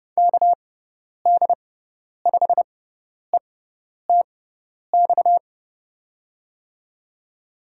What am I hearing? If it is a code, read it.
Morse code, heard as a keyed tone, about 31 wpm, spaced as for 10 wpm, 700 Hz. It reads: KD5ETX
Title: USA Call Signs - 6 Characters - 6x Character Spacing - 30wpm